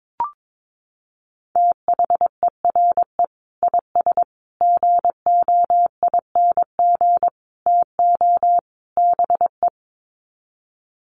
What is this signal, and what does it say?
Morse code, heard as a keyed tone, about 22 wpm, 700 Hz. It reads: THERE IS GOING TO BE
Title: Sets of 5 Words Encore 22wpm